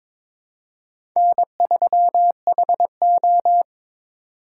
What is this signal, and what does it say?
Morse code, heard as a keyed tone, about 22 wpm, 700 Hz. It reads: N3HO
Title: USA Call Signs - 4 Characters - 2x Character Spacing - 22wpm